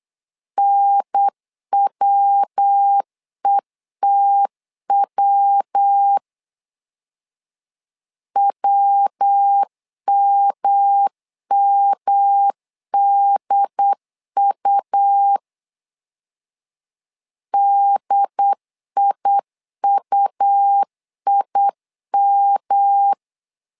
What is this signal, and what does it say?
Morse code, heard as a keyed tone, about 8 wpm, 800 Hz. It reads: NWETW WMMDU DIUIM
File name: LEZIONI / 3 MWRO / 3 MW 8wpm 2501